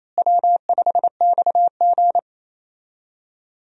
Morse code, heard as a keyed tone, about 28 wpm, 700 Hz. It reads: W5XG
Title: USA Call Signs - 4 Characters - Repeated 2x - 28wpm